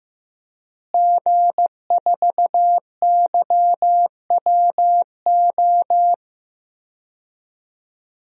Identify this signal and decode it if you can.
Morse code, heard as a keyed tone, about 15 wpm, 700 Hz. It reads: G4YWO
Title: DXCC-100 Call Signs - 5 Characters - Repeated 3x - 15wpm